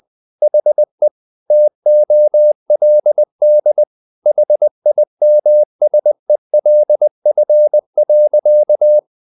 Morse code, heard as a keyed tone, about 20 wpm, 600 Hz. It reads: HE TOLD HIMSELF.
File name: the old man and the sea part 2 20wpm